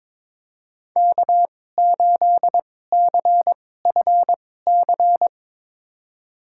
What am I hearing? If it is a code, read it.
Morse code, heard as a keyed tone, about 22 wpm, spaced as for 16 wpm, 700 Hz. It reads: K8CFC
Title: USA Call Signs - 5 Characters - 2x Character Spacing - 22wpm